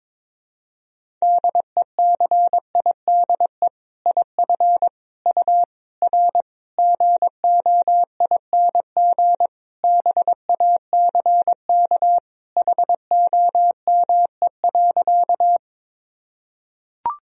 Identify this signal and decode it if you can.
Morse code, heard as a keyed tone, about 22 wpm, 700 Hz. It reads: DECIDE IF U R GOING BACK HOME.
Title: Sentences from Top 500 Words and Abbreviations 22wpm